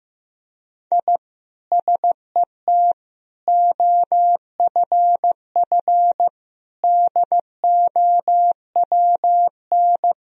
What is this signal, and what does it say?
Morse code, heard as a keyed tone, about 15 wpm, 700 Hz. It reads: I SET OFF DOWN
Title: Sentences from Top 1000 Words - Review Part 1 - 15wpm